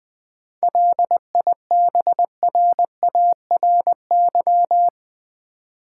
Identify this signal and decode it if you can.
Morse code, heard as a keyed tone, about 20 wpm, 700 Hz. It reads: LIBRARY
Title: Seven Letter Words - 20wpm